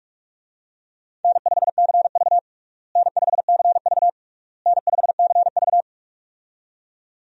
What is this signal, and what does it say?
Morse code, heard as a keyed tone, about 46 wpm, 700 Hz. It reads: N5CV N5CV N5CV
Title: USA Call Signs - 4 Characters - Repeated 3x - 45wpm